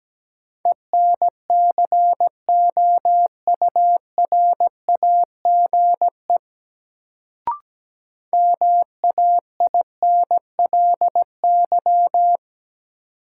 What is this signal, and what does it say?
Morse code, heard as a keyed tone, about 17 wpm, 700 Hz. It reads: ENCOURAGE MAINLY
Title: Sentences from Top 1400 Words - New Word Drill Alt - 17wpm